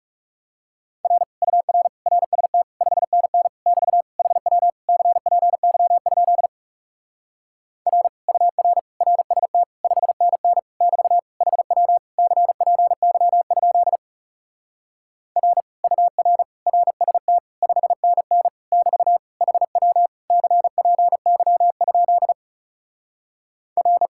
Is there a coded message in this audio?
Yes: Morse code from 1.0 to 24.1 s, heard as a keyed tone, about 40 wpm, 700 Hz. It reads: R UR RST 5NN <BT> HW CPY? R UR RST 5NN <BT> HW CPY? R UR RST 5NN <BT> HW CPY? R